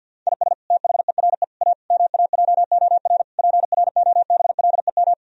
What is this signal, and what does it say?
Morse code, heard as a keyed tone, about 50 wpm, 700 Hz. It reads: IS THERE A MAJOR PROBLEM
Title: Sentences from Top 1000 Words - Review Part 4 - 50wpm